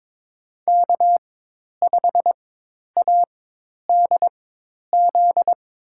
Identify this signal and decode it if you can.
Morse code, heard as a keyed tone, about 21 wpm, spaced as for 10 wpm, 700 Hz. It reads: K5ADZ
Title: USA Call Signs - 5 Characters - 4x Character Spacing - 22wpm